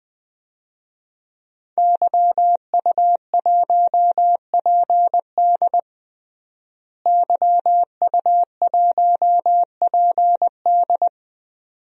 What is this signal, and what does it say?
Morse code, heard as a keyed tone, about 20 wpm, 700 Hz. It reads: YU1PD YU1PD
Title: DXCC-100 Call Signs - 5 Characters - Repeated 2x - 20wpm